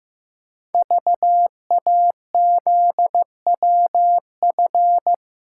Morse code, heard as a keyed tone, about 15 wpm, 700 Hz. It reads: VAZWF
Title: Code Groups - 5 Letter:Numbers - 15wpm